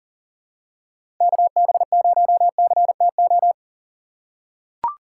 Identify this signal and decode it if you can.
Morse code, heard as a keyed tone, about 38 wpm, 700 Hz. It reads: KB0CTO
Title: USA Call Signs - 6 Characters - 6x Character Spacing - 40wpm